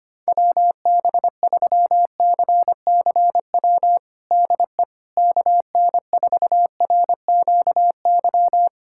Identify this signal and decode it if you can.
Morse code, heard as a keyed tone, about 25 wpm, 700 Hz. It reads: WB3CCW DE KN4RQY
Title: USA Callsign DE Callsign - 6 Characters - 25wpm